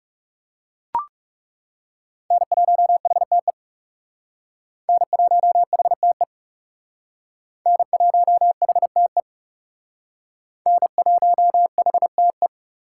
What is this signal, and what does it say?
Morse code, heard as a keyed tone, about 35 wpm, 700 Hz. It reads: N1HTE N1HTE N1HTE N1HTE
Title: Speed Racing - USA 1x3 Call Signs 30wpm